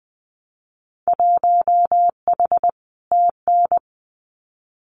Morse code, heard as a keyed tone, about 20 wpm, 700 Hz. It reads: 1H TN
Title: Winter Field Day Exchanges 20wpm